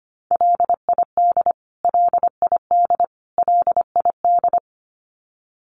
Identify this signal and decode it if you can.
Morse code, heard as a keyed tone, about 25 wpm, 700 Hz. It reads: LID LID LID
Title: Abbreviations Repeated 3x - 25wpm